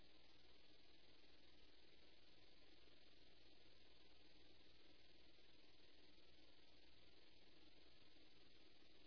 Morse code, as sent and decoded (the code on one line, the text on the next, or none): none